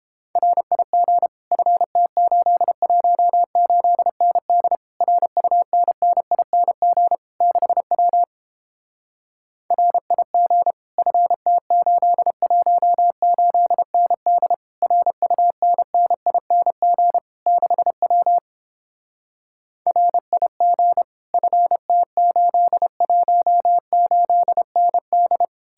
Morse code, heard as a keyed tone, about 29 wpm, 700 Hz. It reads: RIG FT818ND RUNNING 6W RIG FT818ND RUNNING 6W RIG FT818ND